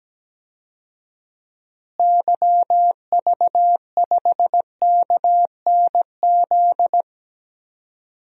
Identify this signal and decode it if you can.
Morse code, heard as a keyed tone, about 17 wpm, 700 Hz. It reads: YV5KNZ
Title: DXCC-100 Call Signs - 6 Characters - Repeated 3x - 17wpm